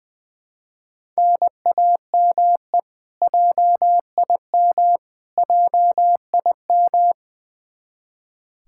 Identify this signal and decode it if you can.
Morse code, heard as a keyed tone, about 20 wpm, 700 Hz. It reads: NAME JIM JIM
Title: Practice QSO - Exchange 1 - Beginning - 20wpm